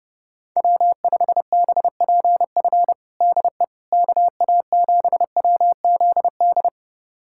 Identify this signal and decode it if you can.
Morse code, heard as a keyed tone, about 30 wpm, 700 Hz. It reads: W5BPF DE KA7WZD
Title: USA Callsign DE Callsign 30wpm